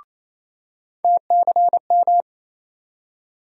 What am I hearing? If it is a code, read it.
Morse code, heard as a keyed tone, about 28 wpm, 700 Hz. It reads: TCM
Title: USA Call Sign Postfixes - 3 Letters 28wpm